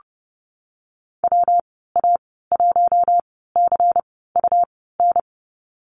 Morse code, heard as a keyed tone, about 30 wpm, spaced as for 17 wpm, 700 Hz. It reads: WA1CUN
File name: Lesson 19g - Letter w - Call Signs 30wpm